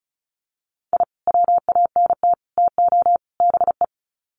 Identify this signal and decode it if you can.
Morse code, heard as a keyed tone, about 34 wpm, 700 Hz. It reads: I WANT TO BE
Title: Sets of 4 Words 35wpm